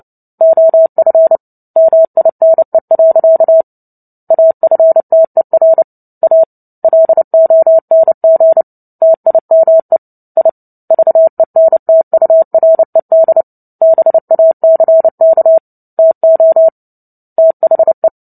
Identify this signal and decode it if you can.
Morse code, heard as a keyed tone, about 30 wpm, 650 Hz. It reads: OF MINE. AFTER A LONG TIME I VENTURED BACK TO THE
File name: Book Two / 03-Chapter 3 - 29 WPM